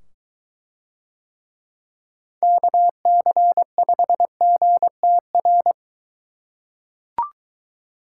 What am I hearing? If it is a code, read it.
Morse code, heard as a keyed tone, about 23 wpm, 700 Hz. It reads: KC5GTR